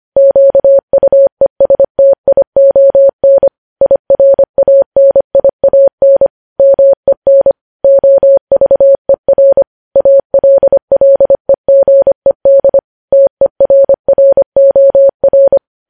Morse code, heard as a keyed tone, about 25 wpm, 550 Hz. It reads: QUESTION IRANIAN MEN OVER ALLEGED TERROR
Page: news-headlines-in-morse-code-at-25-wpm